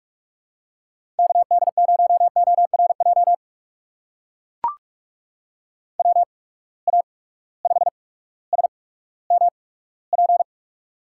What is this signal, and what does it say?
Morse code, heard as a keyed tone, about 44 wpm, spaced as for 19 wpm, 700 Hz. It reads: KD0ORJ WA5SMP
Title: USA Call Signs - 6 Characters - 8x Character Spacing - 45wpm